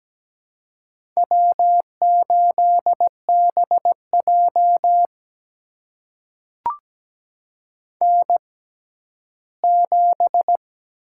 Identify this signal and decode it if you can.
Morse code, heard as a keyed tone, about 17 wpm, spaced as for 11 wpm, 700 Hz. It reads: W8BJ N7
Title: USA Call Signs - 4 Characters - 6x Character Spacing - 17wpm